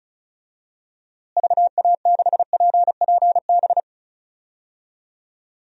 Morse code, heard as a keyed tone, about 36 wpm, 700 Hz. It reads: VA6PPB
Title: DXCC-100 Call Signs - 6 Characters - Repeated 2x - 35wpm